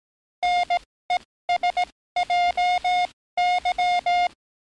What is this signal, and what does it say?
Morse code, heard as a keyed tone, about 18 wpm, spaced as for 14 wpm, 710 Hz. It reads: NESJY
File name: training / NAVPERS 11049to11060 / NAVPERS 11057 RA RB